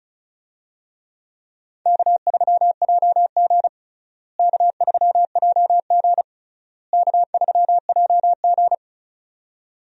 Morse code, heard as a keyed tone, about 34 wpm, 700 Hz. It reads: K3JG K3JG K3JG
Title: USA Call Signs - 4 Characters - Repeated 3x - 35wpm